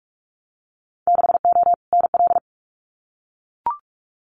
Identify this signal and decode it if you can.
Morse code, heard as a keyed tone, about 44 wpm, 700 Hz. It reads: 6O NL
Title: Winter Field Day Exchanges Encore 45wpm